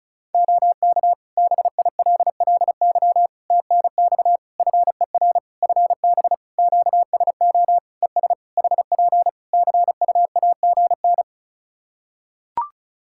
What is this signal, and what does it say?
Morse code, heard as a keyed tone, about 36 wpm, 700 Hz. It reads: OK BILLY TNX FER FB QSO ES HP CUAGN
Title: Practice QSO - Exchange 4 - Ending - 35wpm